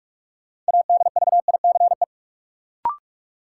Morse code, heard as a keyed tone, about 46 wpm, 700 Hz. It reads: ADVICE
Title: Sentences from Top 1100 Words - New Word Drill Alt - 45wpm